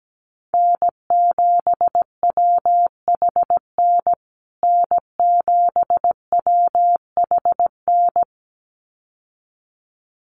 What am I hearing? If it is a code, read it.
Morse code, heard as a keyed tone, about 17 wpm, 700 Hz. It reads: N7WHN N7WHN
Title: USA Call Signs Repeated 2x 17wpm